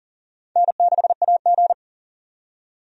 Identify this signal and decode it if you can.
Morse code, heard as a keyed tone, about 40 wpm, 700 Hz. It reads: N6AG
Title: USA Call Signs - 4 Characters - 40wpm